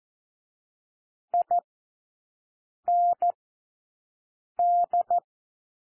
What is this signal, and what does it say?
Morse code, heard as a keyed tone, about 14 wpm, 700 Hz. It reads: I N D
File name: cwmp3 / W1 SOUND WORDS 14 wpm 0